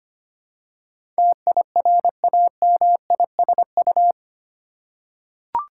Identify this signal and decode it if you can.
Morse code, heard as a keyed tone, about 25 wpm, 700 Hz. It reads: TIRAMISU
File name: practice / 25wpm / Difficult Words with Hints - 25wpm